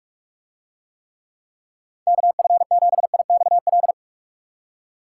Morse code, heard as a keyed tone, about 45 wpm, 700 Hz. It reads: KF7IXL